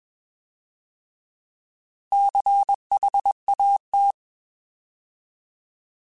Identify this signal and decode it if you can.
Morse code, heard as a keyed tone, about 21 wpm, 800 Hz. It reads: CHAT